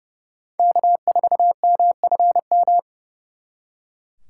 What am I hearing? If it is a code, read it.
Morse code, heard as a keyed tone, about 38 wpm, 700 Hz. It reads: K4MFM